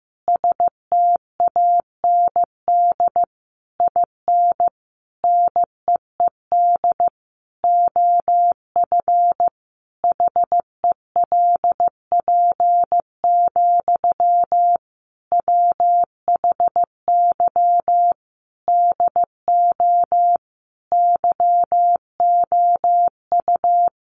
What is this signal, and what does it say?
Morse code, heard as a keyed tone, about 15 wpm, 700 Hz. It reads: STAND IN NEED OF HELP, WHY DO YOU
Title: Sentences from Top 1200 Words 15wpm